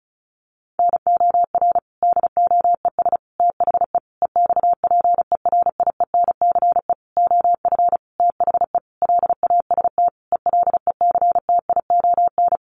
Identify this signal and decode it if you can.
Morse code, heard as a keyed tone, about 34 wpm, 700 Hz. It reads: NOR DOES THE EXPERIENCE OF THE LAST ELECTION
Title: Sentences from Top 1500 Words 35wpm